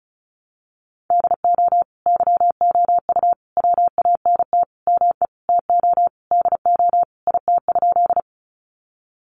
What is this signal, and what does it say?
Morse code, heard as a keyed tone, about 34 wpm, 700 Hz. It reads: DO YOU WANT ME TO DO IT?
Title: Sentences from Top 200 Words 35wpm